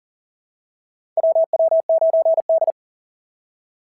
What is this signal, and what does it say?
Morse code, heard as a keyed tone, about 40 wpm, 690 Hz. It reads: WW9D
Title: SOTA QSO Practice 40wpm